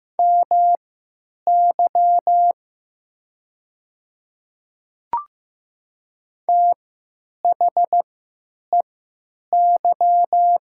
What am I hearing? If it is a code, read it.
Morse code, heard as a keyed tone, about 15 wpm, spaced as for 9 wpm, 690 Hz. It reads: MY THEY
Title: 3x Character Spacing - Top 100 Words - 15wpm